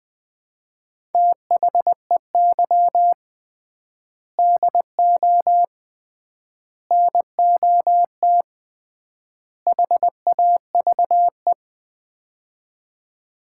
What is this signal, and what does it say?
Morse code, heard as a keyed tone, about 20 wpm, 700 Hz. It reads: THEY DO NOT HAVE
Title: 3x Word-Spacing - Sets of 4 Words - 20wpm